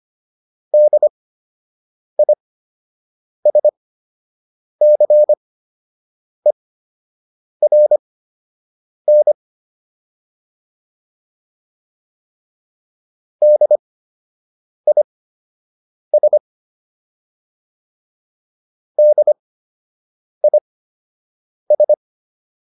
Morse code, heard as a keyed tone, about 25 wpm, spaced as for 7 wpm, 600 Hz. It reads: DISCERN DIS DIS
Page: fundamental-practice-files